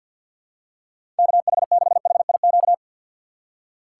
Morse code, heard as a keyed tone, about 50 wpm, 700 Hz. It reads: KH6HIQ
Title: USA Call Signs - 6 Characters - 3x Character Spacing - 50wpm